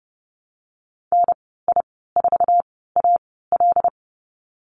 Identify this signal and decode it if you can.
Morse code, heard as a keyed tone, about 32 wpm, spaced as for 17 wpm, 700 Hz. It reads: NI4AL
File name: Lesson 15g - Number 5 - Call Signs 30wpm